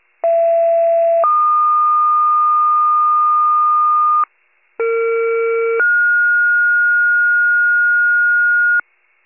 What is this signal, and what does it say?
Two-tone page, heard as a one-second tone, then a three-second tone, about 670 then 1220 Hz, about 460 then 1430 Hz, with noise. It is heard from 0.2 to 8.8 s.